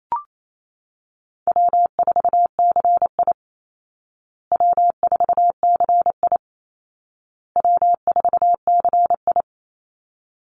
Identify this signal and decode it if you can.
Morse code, heard as a keyed tone, about 28 wpm, 700 Hz. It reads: W4CI W4CI W4CI
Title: Call Signs Repeated 2x - 28wpm